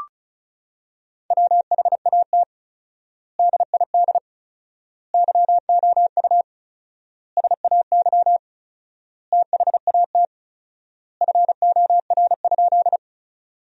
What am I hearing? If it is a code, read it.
Morse code, heard as a keyed tone, about 36 wpm, 700 Hz. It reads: WHAT DID YOU SAY THAT FOR?
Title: 4x Word-Spacing - Sentences from Top 200 Words - 35wpm